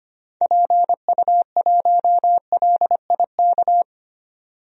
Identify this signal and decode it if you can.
Morse code, heard as a keyed tone, about 25 wpm, 700 Hz. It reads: PU1LIK
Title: DXCC-100 Call Signs - 6 Characters - 25wpm